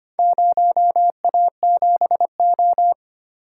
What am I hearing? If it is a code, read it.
Morse code, heard as a keyed tone, about 25 wpm, 700 Hz. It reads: 0A7O
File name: practice / 25wpm / Four Letter-Number Combinations - 25wpm